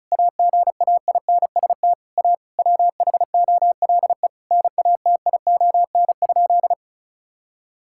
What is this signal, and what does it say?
Morse code, heard as a keyed tone, about 34 wpm, 700 Hz. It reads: AGAINST A WHOLE NATION?
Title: Sentences from Top 1000 Words - Review Part 6 - 35wpm